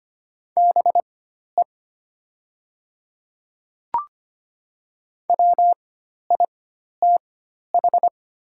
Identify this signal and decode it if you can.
Morse code, heard as a keyed tone, about 25 wpm, spaced as for 12 wpm, 690 Hz. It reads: BE WITH
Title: 4x Character Spacing - Top 200 Words - 25wpm